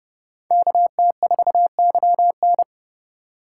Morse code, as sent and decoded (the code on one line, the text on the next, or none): -.- - ....- -.-- -.
KT4YN